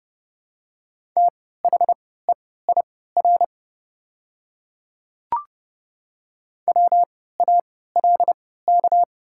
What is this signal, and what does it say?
Morse code, heard as a keyed tone, about 29 wpm, spaced as for 17 wpm, 690 Hz. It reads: THEIR WALK
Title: 3x Character Spacing - Top 300 Words - 30wpm